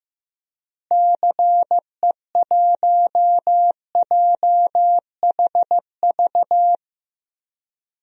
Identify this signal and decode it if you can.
Morse code, heard as a keyed tone, about 16 wpm, 700 Hz. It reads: CE1JHV
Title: DXCC-100 Call Signs - 6 Characters - Repeated 3x - 15wpm